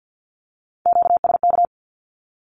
Morse code, heard as a keyed tone, about 50 wpm, 700 Hz. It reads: QSK
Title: Q-Codes with Spoken Meaning - 50wpm